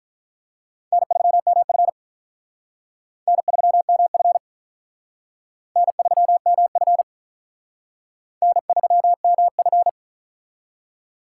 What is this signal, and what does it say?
Morse code, heard as a keyed tone, about 44 wpm, 700 Hz. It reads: N3MF N3MF N3MF N3MF